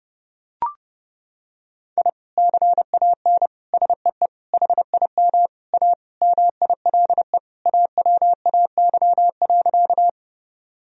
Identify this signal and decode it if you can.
Morse code, heard as a keyed tone, about 30 wpm, 700 Hz. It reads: I CAN SEE HIM A MILE AWAY.
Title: Sentences from Top 1000 Words - Review Part 1 - 30wpm